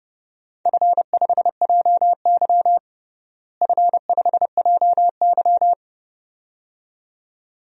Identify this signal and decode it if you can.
Morse code, heard as a keyed tone, about 30 wpm, 700 Hz. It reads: F5JY F5JY
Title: DXCC-100 Call Signs - 4 Characters - Repeated 2x - 30wpm